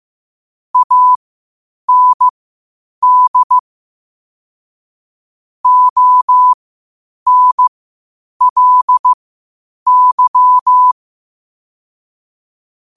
Morse code, heard as a keyed tone, about 15 wpm, spaced as for 8 wpm, 1000 Hz. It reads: AND ONLY
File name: qotdfiles / 8wpm